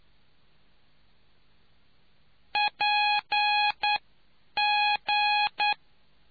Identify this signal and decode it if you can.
Morse code, heard as a keyed tone, about 9 wpm, spaced as for 8 wpm, 830 Hz. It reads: PG